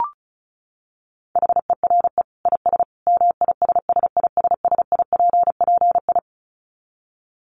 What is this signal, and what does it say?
Morse code, heard as a keyed tone, about 34 wpm, 700 Hz. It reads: HERE IS MISSISSIPPI
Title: Dit Challenge 35wpm